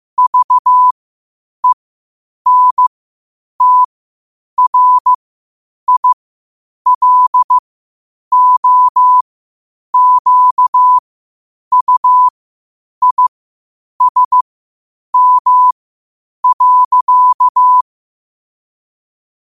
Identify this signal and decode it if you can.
Morse code, heard as a keyed tone, about 15 wpm, spaced as for 8 wpm, 1000 Hz. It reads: VENTRILOQUISM.
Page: New quotes every day in morse code at 8 Words per minute.